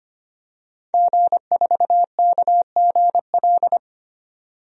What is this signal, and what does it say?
Morse code, heard as a keyed tone, about 25 wpm, 700 Hz. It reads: G4KGL
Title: DXCC-100 Call Signs - 5 Characters - 25wpm